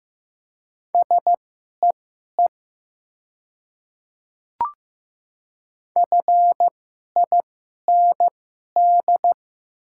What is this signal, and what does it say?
Morse code, heard as a keyed tone, about 15 wpm, spaced as for 11 wpm, 690 Hz. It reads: SEE FIND
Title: 2x Character Spacing - Top 100 Words - 15wpm